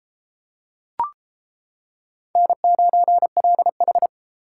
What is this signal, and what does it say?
Morse code, heard as a keyed tone, about 33 wpm, 700 Hz. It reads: N9LH